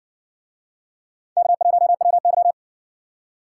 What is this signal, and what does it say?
Morse code, heard as a keyed tone, about 60 wpm, 700 Hz. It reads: K1WY